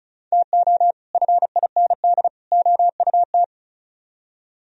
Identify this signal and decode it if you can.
Morse code, heard as a keyed tone, about 46 wpm, 700 Hz. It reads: TO FIND OUT